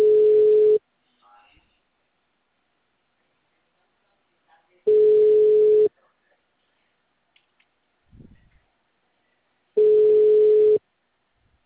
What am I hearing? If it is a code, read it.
Ringback tone (Europe, 425 Hz), with speech.
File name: out- -2012- - - .25731